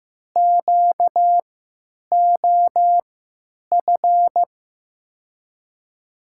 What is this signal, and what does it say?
Morse code, heard as a keyed tone, about 15 wpm, spaced as for 9 wpm, 700 Hz. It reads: QOF
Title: USA Call Signs - 6 Characters - 3x Character Spacing - 15wpm